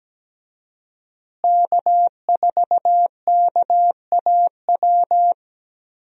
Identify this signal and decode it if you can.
Morse code, heard as a keyed tone, about 17 wpm, 700 Hz. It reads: K4KAW
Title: USA Call Signs - 5 Characters - Repeated 2x - 17wpm